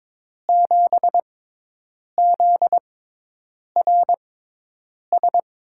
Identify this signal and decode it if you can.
Morse code, heard as a keyed tone, about 22 wpm, spaced as for 8 wpm, 700 Hz. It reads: 7ZRS
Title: USA Call Signs - 6 Characters - 6x Character Spacing - 22wpm